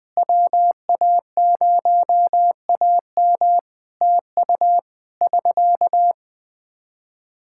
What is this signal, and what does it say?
Morse code, heard as a keyed tone, about 20 wpm, 700 Hz. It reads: WA0AM TU <SK>
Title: Practice QSO - Exchange 4 - Ending - 20wpm